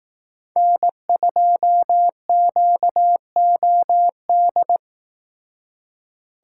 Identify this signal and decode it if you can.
Morse code, heard as a keyed tone, about 18 wpm, 700 Hz. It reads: N2QOD
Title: Speed Racing - USA 1x3 Call Signs 15wpm